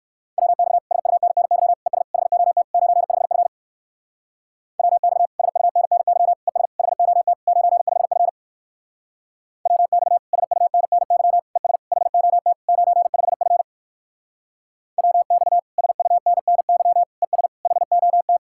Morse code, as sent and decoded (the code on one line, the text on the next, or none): .-- -..- / ... ..- -. -. -.-- / . ... / .... --- - / ----. ..... ..-. / .-- -..- / ... ..- -. -. -.-- / . ... / .... --- - / ----. ..... ..-. / .-- -..- / ... ..- -. -. -.-- / . ... / .... --- - / ----. ..... ..-. / .-- -..- / ... ..- -. -. -.-- / . ... / .... --- -
WX SUNNY ES HOT 95F WX SUNNY ES HOT 95F WX SUNNY ES HOT 95F WX SUNNY ES HOT